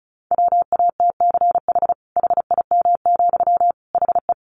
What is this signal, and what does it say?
Morse code, heard as a keyed tone, about 34 wpm, 700 Hz. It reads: WATCH HIM, HE
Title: Sentences from Top 300 Words 35wpm